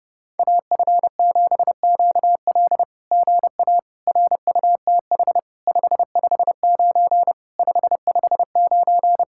Morse code, heard as a keyed tone, about 30 wpm, 700 Hz. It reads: AF7QL GA RUTH 559 559
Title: SOTA QSO Practice 30wpm